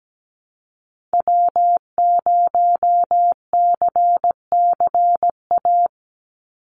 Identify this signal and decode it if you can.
Morse code, heard as a keyed tone, about 17 wpm, 700 Hz. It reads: W0CCA
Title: NA SOTA Activator Callsigns 17wpm